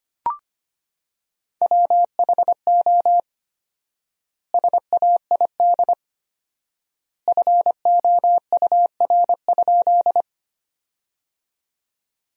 Morse code, heard as a keyed tone, about 25 wpm, 700 Hz. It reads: WHO SAID FOUR?
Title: 4x Word-Spacing - Sentences from Top 300 Words - 25wpm